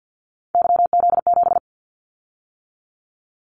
Morse code, heard as a keyed tone, about 50 wpm, 700 Hz. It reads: YZ7
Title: Callsign Prefixes for Top 25 DXCC - 50wpm